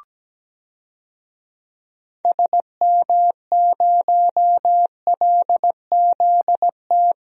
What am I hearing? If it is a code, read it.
Morse code, heard as a keyed tone, about 17 wpm, 700 Hz. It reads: SM0LZT
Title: DXCC-100 Call Signs - 6 Characters - 17wpm